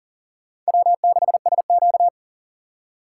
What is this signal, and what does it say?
Morse code, heard as a keyed tone, about 40 wpm, 700 Hz. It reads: W6SQ
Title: USA Call Signs - 4 Characters - 6x Character Spacing - 40wpm